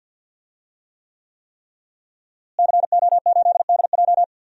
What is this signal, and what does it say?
Morse code, heard as a keyed tone, about 52 wpm, 700 Hz. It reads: CO8DJ
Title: DXCC-100 Call Signs - 5 Characters - Repeated 3x - 50wpm